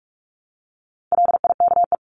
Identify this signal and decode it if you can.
Morse code, heard as a keyed tone, about 44 wpm, 700 Hz. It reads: LIKE
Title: LY Suffix - Top 100 Words with Base Words 45wpm